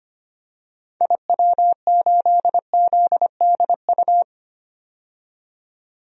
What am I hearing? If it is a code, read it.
Morse code, heard as a keyed tone, about 25 wpm, 700 Hz. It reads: IW8ZDU
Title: DXCC-100 Call Signs - 6 Characters - 25wpm